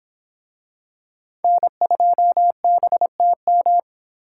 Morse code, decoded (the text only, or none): N2BTM